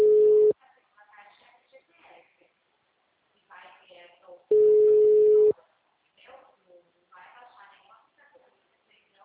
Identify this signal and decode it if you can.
Ringback tone (Europe, 425 Hz), with speech.